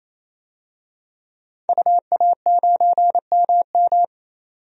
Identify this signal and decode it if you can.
Morse code, heard as a keyed tone, about 27 wpm, 700 Hz. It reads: UA9MM
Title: DXCC-100 Call Signs - 5 Characters - 28wpm